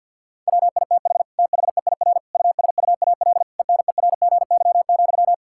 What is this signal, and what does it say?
Morse code, heard as a keyed tone, about 50 wpm, 700 Hz. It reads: WITH THEIR USUAL ENERGY,
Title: Sentences from Top 1000 Words - Review Part 6 - 50wpm